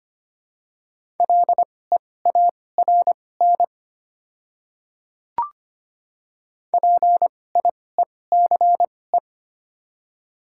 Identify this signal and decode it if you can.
Morse code, heard as a keyed tone, about 25 wpm, spaced as for 18 wpm, 700 Hz. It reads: LEARN PIECE
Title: 2x Character Spacing - Top 500 Words - 25wpm